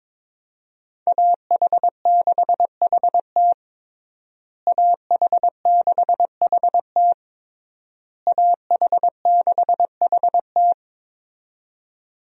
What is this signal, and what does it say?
Morse code, heard as a keyed tone, about 22 wpm, 700 Hz. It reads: AH6HT AH6HT AH6HT
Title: USA Call Signs - 5 Characters - Repeated 3x - 22wpm